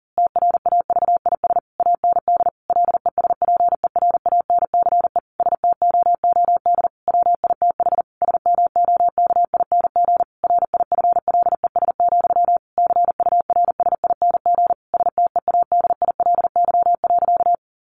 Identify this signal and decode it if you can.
Morse code, heard as a keyed tone, about 40 wpm, 700 Hz. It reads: TRAVIS AND LESPERANCE STOOD WITH SMOKING RIFLES, CURSING STEADILY.
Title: A Sound Of Thunder 40wpm